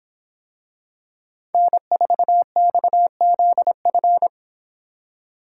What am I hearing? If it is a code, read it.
Morse code, heard as a keyed tone, about 26 wpm, 700 Hz. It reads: N4XZF